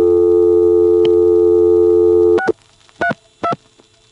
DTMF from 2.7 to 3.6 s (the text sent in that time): 32